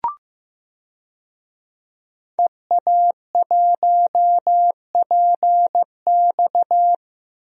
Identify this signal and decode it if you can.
Morse code, heard as a keyed tone, about 16 wpm, 700 Hz. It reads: EA1PX
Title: DXCC-100 Call Signs - 5 Characters - Repeated 2x - 15wpm